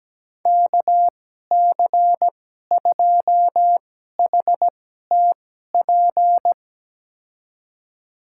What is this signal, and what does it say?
Morse code, heard as a keyed tone, about 17 wpm, spaced as for 12 wpm, 700 Hz. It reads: KC2HTP
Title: USA Call Signs - 6 Characters - 2x Character Spacing - 17wpm